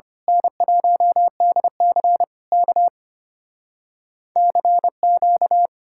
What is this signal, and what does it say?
Morse code, heard as a keyed tone, about 24 wpm, 700 Hz. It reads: N1DC K CQ
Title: Speed-Racing Common QSO Phrases - 25wpm